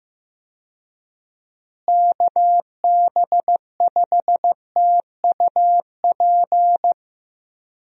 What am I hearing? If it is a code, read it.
Morse code, heard as a keyed tone, about 15 wpm, 700 Hz. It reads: KB5TUP
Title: USA Call Signs - 6 Characters - Repeated 2x - 15wpm